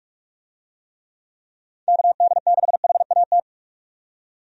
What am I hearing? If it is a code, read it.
Morse code, heard as a keyed tone, about 46 wpm, 700 Hz. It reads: KD6HAT
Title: USA Call Signs - 6 Characters - 45wpm